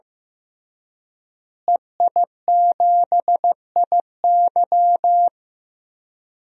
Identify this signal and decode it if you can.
Morse code, heard as a keyed tone, about 15 wpm, 700 Hz. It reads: EI7IY
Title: DXCC-100 Call Signs - 5 Characters - Repeated 3x - 15wpm